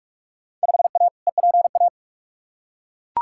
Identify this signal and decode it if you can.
Morse code, heard as a keyed tone, about 42 wpm, 700 Hz. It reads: 5A EPA
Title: Field Day Exchanges - 45wpm